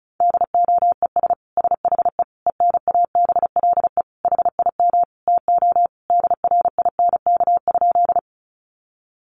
Morse code, heard as a keyed tone, about 34 wpm, 700 Hz. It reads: DOES SHE ENABLE HIM TO DRINK?
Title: Sentences from Top 1500 Words 35wpm